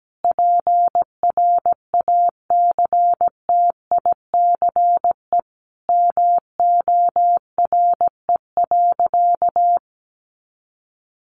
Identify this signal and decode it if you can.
Morse code, heard as a keyed tone, about 17 wpm, 700 Hz. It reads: PRACTICE MORE.
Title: Sentences from Top 700 Words 17wpm